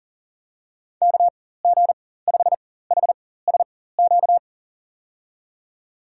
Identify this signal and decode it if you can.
Morse code, heard as a keyed tone, about 40 wpm, spaced as for 19 wpm, 700 Hz. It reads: KG5HSQ
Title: USA Call Signs - 4x Character Spacing - 40wpm